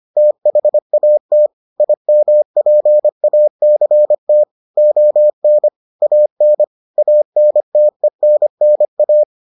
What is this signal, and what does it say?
Morse code, heard as a keyed tone, about 25 wpm, 600 Hz. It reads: THAT IMPACT ON AN ANTENNA